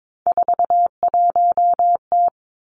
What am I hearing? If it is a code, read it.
Morse code, heard as a keyed tone, about 22 wpm, 700 Hz. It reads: 41T
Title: Three Challenging Letter-Number Combinations 22wpm